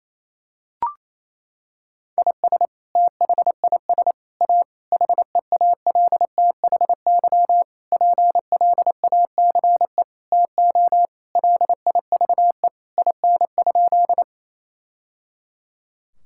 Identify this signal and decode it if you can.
Morse code, heard as a keyed tone, about 27 wpm, 700 Hz. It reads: IS THIS A HEALTHY PLACE TO LIVE IN?
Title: Sentences from Top 1000 Words - Review Part 5 - 28wpm